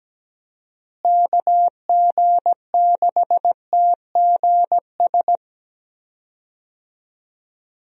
Morse code, heard as a keyed tone, about 17 wpm, 700 Hz. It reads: KG6TGS
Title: USA Call Signs - 6 Characters - Repeated 2x - 17wpm